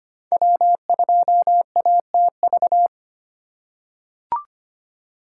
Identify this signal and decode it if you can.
Morse code, heard as a keyed tone, about 25 wpm, 700 Hz. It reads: W2ATV
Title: USA Call Signs - 3x Character Spacing - 25wpm